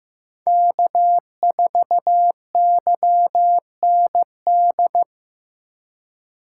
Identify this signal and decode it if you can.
Morse code, heard as a keyed tone, about 15 wpm, 700 Hz. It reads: K4YND
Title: K1USN SST QSO Practice - 15wpm